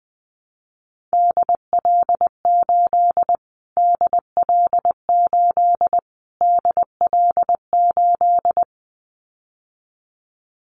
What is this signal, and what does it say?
Morse code, heard as a keyed tone, about 20 wpm, 700 Hz. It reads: DL8 DL8 DL8
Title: Callsign Prefixes for Top 25 DXCC - Repeated 3x - 20wpm